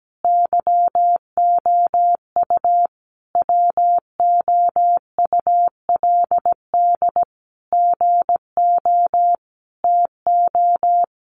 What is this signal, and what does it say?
Morse code, heard as a keyed tone, about 17 wpm, 700 Hz. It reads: YOU WOULD GO TO
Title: Sentences from Top 600 Words 17wpm